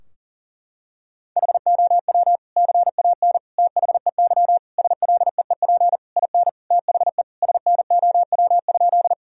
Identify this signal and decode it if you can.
Morse code, heard as a keyed tone, about 40 wpm, 700 Hz. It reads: HOW CAN THEY SLEEP IN THE SNOW?
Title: Sentences from Top 1000 Words - Review Part 3 - 40wpm